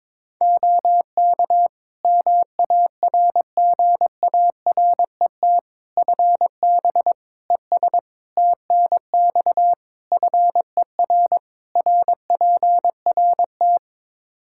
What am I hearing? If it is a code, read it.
Morse code, heard as a keyed tone, about 22 wpm, 700 Hz. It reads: OK MARGARET FB ES TNX FER RPRT
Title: Practice QSO - Exchange 2 - Info - 22wpm